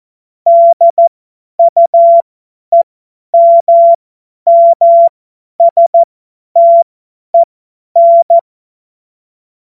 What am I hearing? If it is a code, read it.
Morse code, heard as a keyed tone, about 14 wpm, spaced as for 10 wpm, 680 Hz. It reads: DUEMMSTEN
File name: Morsekurs / Witze / Witz 04. Mai 2021, 50 F70